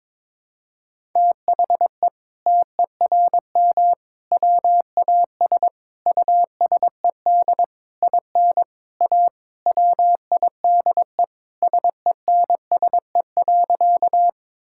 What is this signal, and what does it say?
Morse code, heard as a keyed tone, about 22 wpm, 700 Hz. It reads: THE TERM WAS USED IN A WIDE SENSE.
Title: Sentences from Top 1000 Words - Review Part 6 - 22wpm